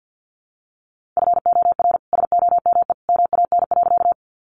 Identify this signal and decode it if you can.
Morse code, heard as a keyed tone, about 50 wpm, 700 Hz. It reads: FOR SOME MAN.
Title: Sentences from Top 200 Words 50wpm